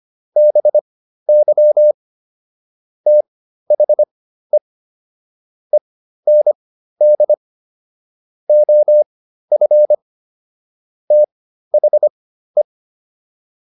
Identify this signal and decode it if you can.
Morse code, heard as a keyed tone, about 25 wpm, spaced as for 13 wpm, 600 Hz. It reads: BY THE END OF THE